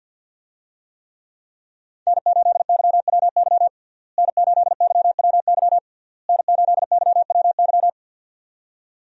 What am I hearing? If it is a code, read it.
Morse code, heard as a keyed tone, about 48 wpm, 700 Hz. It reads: N8YWY N8YWY N8YWY
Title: USA Call Signs - 5 Characters - Repeated 3x - 50wpm